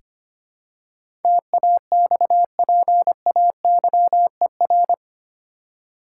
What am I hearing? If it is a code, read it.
Morse code, heard as a keyed tone, about 25 wpm, 700 Hz. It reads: TAXPAYER
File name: 8 Letter Words - First 3 Letters Given - 25wpm